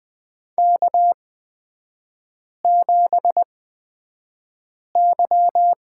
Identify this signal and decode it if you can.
Morse code, heard as a keyed tone, about 20 wpm, spaced as for 5 wpm, 700 Hz. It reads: K7Y
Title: 236 Lesson 38e - Letter q - Three Characters 20wpm